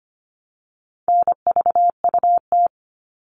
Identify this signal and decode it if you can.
Morse code, heard as a keyed tone, about 25 wpm, 700 Hz. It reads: NVUT
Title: All Cut Numbers - 4-Digit 25wpm